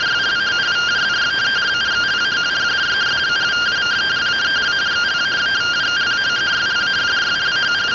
Two-tone teleprinter signal, about 170 Hz shift.